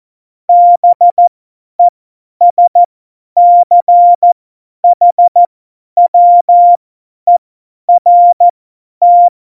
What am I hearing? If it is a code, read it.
Morse code, heard as a keyed tone, about 14 wpm, spaced as for 10 wpm, 700 Hz. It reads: BESCHWERT